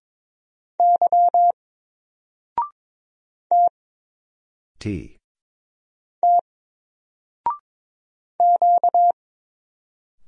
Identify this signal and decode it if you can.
Morse code, heard as a keyed tone, about 22 wpm, 700 Hz. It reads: Y T T Q
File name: Single-Letter-Number-22wpm